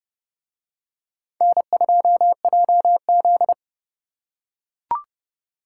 Morse code, heard as a keyed tone, about 28 wpm, 700 Hz. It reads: N2JZ